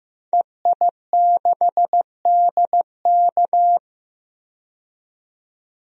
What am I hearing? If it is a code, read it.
Morse code, heard as a keyed tone, about 15 wpm, 700 Hz. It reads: EI6DK
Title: DXCC-100 Call Signs - 5 Characters - 15wpm